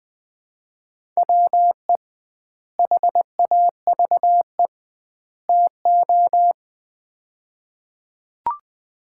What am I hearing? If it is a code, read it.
Morse code, heard as a keyed tone, about 20 wpm, 700 Hz. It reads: WE HAVE TO
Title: 2x Word-Spacing - Sets of 3 Words - 20wpm